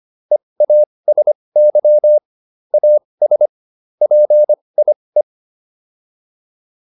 Morse code, heard as a keyed tone, about 25 wpm, spaced as for 20 wpm, 600 Hz. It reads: EASY AS PIE